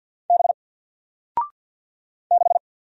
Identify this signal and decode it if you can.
Morse code, heard as a keyed tone, about 49 wpm, 700 Hz. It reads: B 6
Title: Single Letter-Number 50wpm